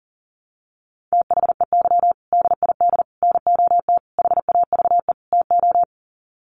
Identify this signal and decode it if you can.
Morse code, heard as a keyed tone, about 40 wpm, 700 Hz. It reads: THEY DID NOT HAVE TO
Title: Sets of 5 Words Encore2 40wpm